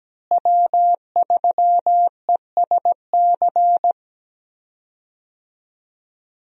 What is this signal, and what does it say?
Morse code, heard as a keyed tone, about 17 wpm, 700 Hz. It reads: W3ESC
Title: USA Call Signs - 5 Characters - Repeated 2x - 17wpm